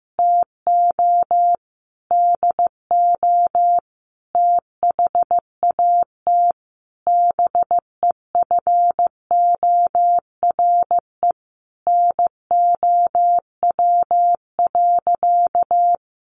Morse code, heard as a keyed tone, about 15 wpm, 700 Hz. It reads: TO DO THAT BEFORE NOW.
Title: Sentences from Top 1300 Words 15wpm